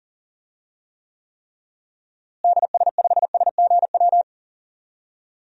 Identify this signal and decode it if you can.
Morse code, heard as a keyed tone, about 40 wpm, 700 Hz. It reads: DS5SGW
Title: DXCC-100 Call Signs - 6 Characters - Repeated 3x - 40wpm